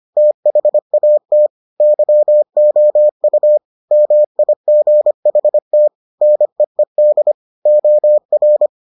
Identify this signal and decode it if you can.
Morse code, heard as a keyed tone, about 25 wpm, 600 Hz. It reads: THAT YOU MIGHT NEED OR